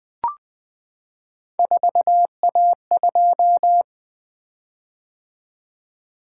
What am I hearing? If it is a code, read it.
Morse code, heard as a keyed tone, about 20 wpm, 700 Hz. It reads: 4A2
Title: Three Letter-Number Combinations - 20wpm